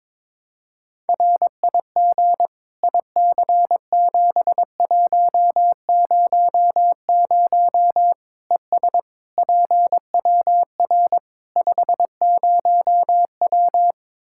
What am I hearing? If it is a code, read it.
Morse code, heard as a keyed tone, about 22 wpm, 700 Hz. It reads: RIG IC7100 ES PWR 50W
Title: Practice QSO - Exchange 2 - Info - 22wpm